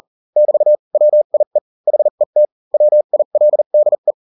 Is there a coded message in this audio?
Yes: Morse code from 0.4 to 4.1 s, heard as a keyed tone, about 40 wpm, spaced as for 35 wpm, 600 Hz. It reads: <BT> WIE HET WILDE